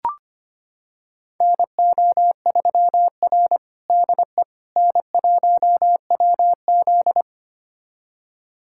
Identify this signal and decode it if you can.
Morse code, heard as a keyed tone, about 25 wpm, 700 Hz. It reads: NO3R DE N1WZ
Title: USA Callsign DE Callsign - 4 Characters - 25wpm